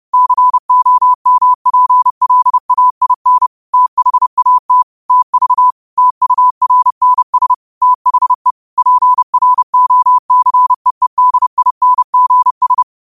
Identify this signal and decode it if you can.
Morse code, heard as a keyed tone, about 30 wpm, 1000 Hz. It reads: COMPLAIN THAT TV TURNS THE PROCEEDINGS
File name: qotdfiles / 30wpm